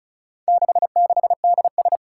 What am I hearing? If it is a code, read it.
Morse code, heard as a keyed tone, about 36 wpm, 700 Hz. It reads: 66DS
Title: Four Challenging Letter-Number Combinations 35wpm